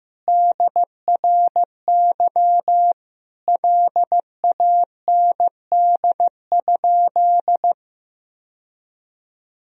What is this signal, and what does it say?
Morse code, heard as a keyed tone, about 16 wpm, 700 Hz. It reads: DRY LAND?
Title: Sentences from Top 1000 Words - Review Part 3 - 15wpm